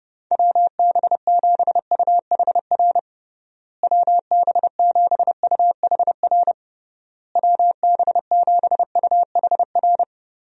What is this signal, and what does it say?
Morse code, heard as a keyed tone, about 30 wpm, 700 Hz. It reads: WB7UHR WB7UHR WB7UHR
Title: USA Call Signs - 6 Characters - Repeated 3x - 30wpm